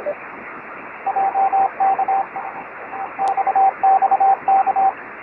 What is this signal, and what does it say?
Morse code, heard as a keyed tone, about 25 wpm, 780 Hz, with radio noise. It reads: JK XXK
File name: RS44 UA9FDZ